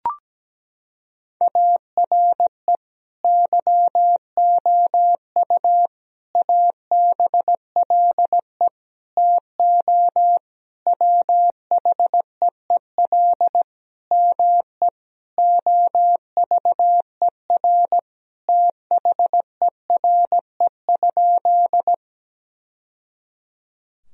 Morse code, heard as a keyed tone, about 17 wpm, 700 Hz. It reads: ARE YOU ABLE TO WHEEL ME OVER THERE?
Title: Sentences from Top 1000 Words - Review Part 3 - 17wpm